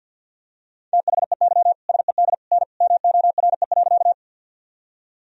Morse code, heard as a keyed tone, about 47 wpm, 700 Hz. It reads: THEY SED N MORE.
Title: Sentences from Top 100 Words and Abbreviations 50wpm